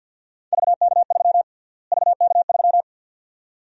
Morse code, heard as a keyed tone, about 49 wpm, 700 Hz. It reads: VK3 VK3
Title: USA Call Sign Prefixes - Repeated 2x - 50wpm